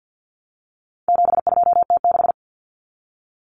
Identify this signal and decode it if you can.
Morse code, heard as a keyed tone, about 50 wpm, 700 Hz. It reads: 72T6
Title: Common Cut Numbers - 4-Digit 50wpm